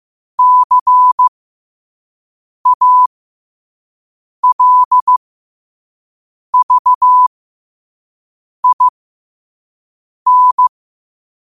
Morse code, heard as a keyed tone, about 15 wpm, spaced as for 5 wpm, 1000 Hz. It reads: CALVIN
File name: qotdfiles / 5wpm